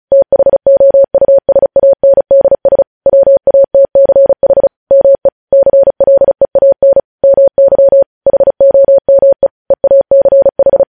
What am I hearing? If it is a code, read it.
Morse code, heard as a keyed tone, about 36 wpm, 550 Hz. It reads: THOUSANDS WATCH ME CLEAN MY HOME EACH
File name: podcast / News headlines-1x-35x35